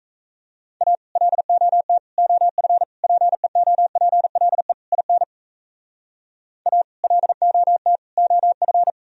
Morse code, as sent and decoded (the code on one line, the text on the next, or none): .- / .-.. --- - / --- ..-. / .--. --- .--. .-.. / .. -. / .- / .-.. --- - / --- ..-.
A LOT OF PEOPLE IN A LOT OF